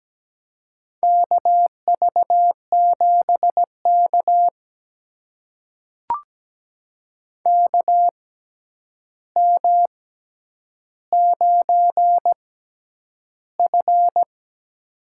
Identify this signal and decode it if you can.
Morse code, heard as a keyed tone, about 17 wpm, spaced as for 9 wpm, 700 Hz. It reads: KV7K KM9F
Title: USA Call Signs - 4 Characters - 6x Character Spacing - 17wpm